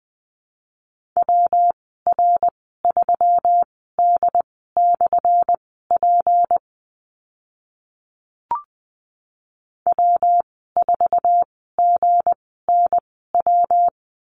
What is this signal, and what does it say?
Morse code, heard as a keyed tone, about 20 wpm, spaced as for 14 wpm, 700 Hz. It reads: WR3D/P W4GNW
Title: 179 Lesson 29g - Slash - Call Signs 20wpm